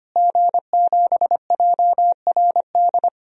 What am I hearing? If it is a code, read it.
Morse code, heard as a keyed tone, about 25 wpm, 700 Hz. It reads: G7JRD
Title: DXCC-100 Call Signs - 5 Characters - Repeated 3x - 25wpm